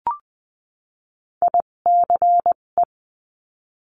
Morse code, heard as a keyed tone, about 20 wpm, spaced as for 17 wpm, 700 Hz. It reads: ICE
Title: 257 Lesson 42b - Words 14-20wpm